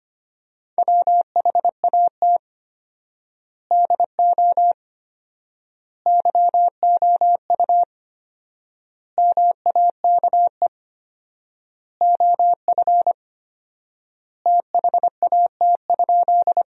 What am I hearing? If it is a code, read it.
Morse code, heard as a keyed tone, about 25 wpm, 700 Hz. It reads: WHAT DO YOU MAKE OF THAT?
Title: 4x Word-Spacing - Sentences from Top 100 Words - 25wpm